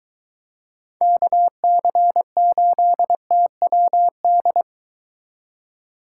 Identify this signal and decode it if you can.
Morse code, heard as a keyed tone, about 22 wpm, 700 Hz. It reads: KC8TWD